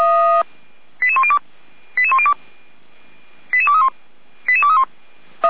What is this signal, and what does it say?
Five-tone selective call (ZVEI), heard as quick runs of five tones, tones of about 70 ms. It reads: 88102 88102 88231 88231